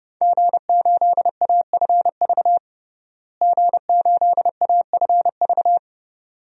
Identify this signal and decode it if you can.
Morse code, heard as a keyed tone, about 30 wpm, 700 Hz. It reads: G8AFV G8AFV
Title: DXCC-100 Call Signs - 5 Characters - Repeated 2x - 30wpm